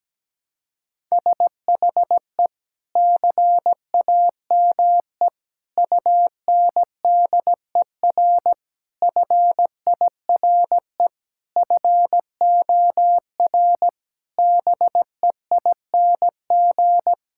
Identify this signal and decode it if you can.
Morse code, heard as a keyed tone, about 17 wpm, 700 Hz. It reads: SHE CAME UNDER FIRE FOR BEING
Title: Sentences from Top 1000 Words - Review Part 6 - 17wpm